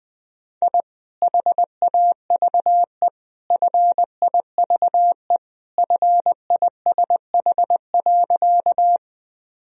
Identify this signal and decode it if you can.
Morse code, heard as a keyed tone, about 20 wpm, 700 Hz. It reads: I HAVE FIVE FISH.
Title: Sentences from Top 1000 Words - Review Part 2 - 20wpm